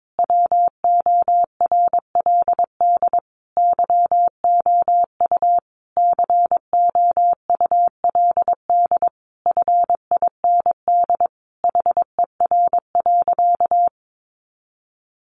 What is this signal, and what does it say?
Morse code, heard as a keyed tone, about 22 wpm, 700 Hz. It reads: WORLD YOU COULD FIND HER.
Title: Sentences from Top 1400 Words 22wpm